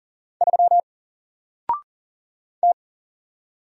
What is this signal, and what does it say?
Morse code, heard as a keyed tone, about 40 wpm, 700 Hz. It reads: 3 T
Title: Single Character (Letter:Number:Punctuation:Prosign) Alt 40wpm